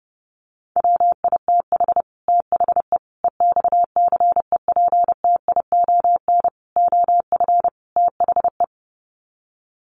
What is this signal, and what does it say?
Morse code, heard as a keyed tone, about 30 wpm, 700 Hz. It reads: WITH THE EXCEPTION OF THE
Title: Sets of 5 Words Encore 30wpm